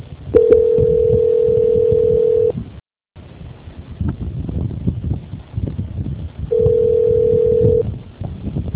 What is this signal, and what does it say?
Ringback tone (North America), 440 + 480 Hz.